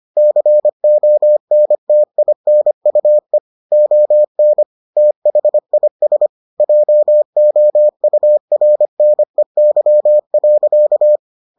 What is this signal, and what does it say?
Morse code, heard as a keyed tone, about 25 wpm, 600 Hz. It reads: CONTINUE ON THIS JOURNEY.